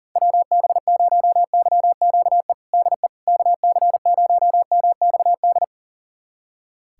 Morse code, heard as a keyed tone, about 40 wpm, 700 Hz. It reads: WB0YQE DE KC0MXD
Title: USA Callsign DE Callsign - 6 Characters - 40wpm